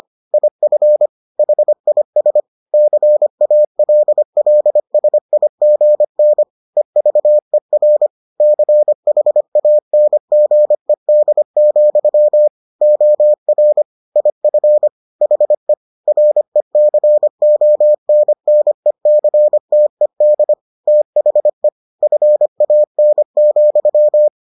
Morse code, heard as a keyed tone, about 25 wpm, 600 Hz. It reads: IF HIS CALLSIGN EVER CHANGED, OR IF HE RECONNECTED THE FAN,